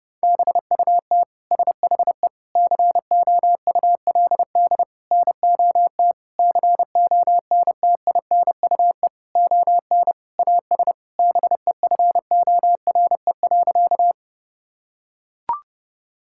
Morse code, heard as a keyed tone, about 30 wpm, 700 Hz. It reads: BUT SHE COULD NOT CONTINUE ON AS BEFORE.
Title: Sentences from Top 1000 Words - Review Part 6 - 30wpm